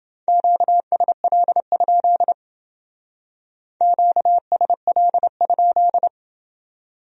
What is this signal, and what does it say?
Morse code, heard as a keyed tone, about 29 wpm, 700 Hz. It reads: QSL? QSL?